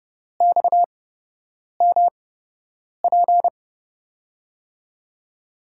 Morse code, heard as a keyed tone, about 30 wpm, spaced as for 8 wpm, 700 Hz. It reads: XMP
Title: USA Call Signs - 5 Characters - 8x Character Spacing - 30wpm